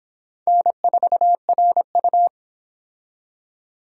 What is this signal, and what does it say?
Morse code, heard as a keyed tone, about 24 wpm, 700 Hz. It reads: N4RU